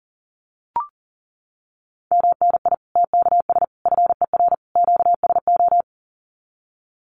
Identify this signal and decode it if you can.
Morse code, heard as a keyed tone, about 40 wpm, 700 Hz. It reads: MNI TKS FER QSO
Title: Common QSO Phrases 40wpm